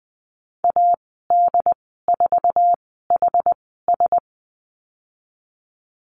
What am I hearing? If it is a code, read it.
Morse code, heard as a keyed tone, about 20 wpm, spaced as for 15 wpm, 700 Hz. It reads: AD4HS
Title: 069 Lesson 12g - Letter d - Call Signs 20wpm